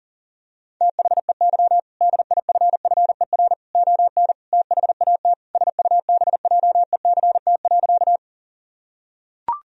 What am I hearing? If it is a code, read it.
Morse code, heard as a keyed tone, about 40 wpm, 700 Hz. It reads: THEY DIFFER ON THAT SUBJECT.
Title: Sentences from Top 1000 Words - Review Part 1 - 40wpm